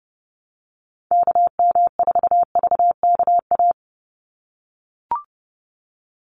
Morse code, heard as a keyed tone, about 30 wpm, 700 Hz. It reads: KM4VKA
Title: USA Call Signs 30wpm